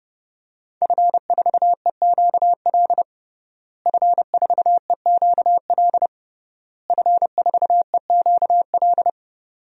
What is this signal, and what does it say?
Morse code, heard as a keyed tone, about 30 wpm, 700 Hz. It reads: F4EQL F4EQL F4EQL
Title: DXCC-100 Call Signs - 5 Characters - Repeated 3x - 30wpm